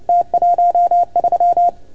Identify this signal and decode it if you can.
Morse code, heard as a keyed tone, about 30 wpm, 690 Hz. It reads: T13
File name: Archiv / CUC310 / OK1FGD CUC 310 pileup